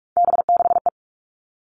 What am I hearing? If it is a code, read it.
Morse code, heard as a keyed tone, about 44 wpm, 700 Hz. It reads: B6E
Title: All Cut Numbers - 3-Digit 45wpm